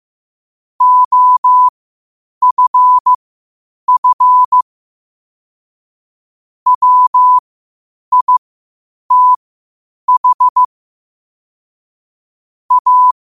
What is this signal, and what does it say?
Morse code, heard as a keyed tone, about 15 wpm, spaced as for 8 wpm, 1000 Hz. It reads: OFF WITH A
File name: qotdfiles / 8wpm